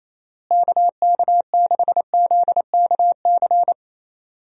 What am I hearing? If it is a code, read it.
Morse code, heard as a keyed tone, about 28 wpm, 700 Hz. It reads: KK6ZKC
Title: USA Call Signs - 6 Characters - Repeated 3x - 28wpm